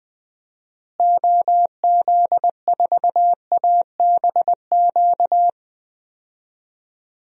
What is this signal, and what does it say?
Morse code, heard as a keyed tone, about 21 wpm, 700 Hz. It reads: OZ4ABQ
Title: DXCC-100 Call Signs - 6 Characters - Repeated 2x - 20wpm